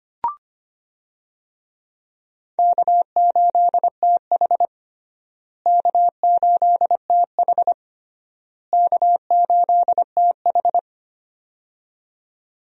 Morse code, heard as a keyed tone, about 25 wpm, 700 Hz. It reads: K8TH K8TH K8TH
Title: USA Call Signs - 4 Characters - Repeated 3x - 25wpm